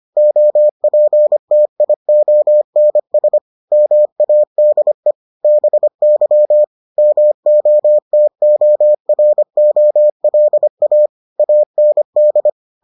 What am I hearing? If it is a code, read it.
Morse code, heard as a keyed tone, about 25 wpm, 600 Hz. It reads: OPTIONS MADE BY MOTOROLA AND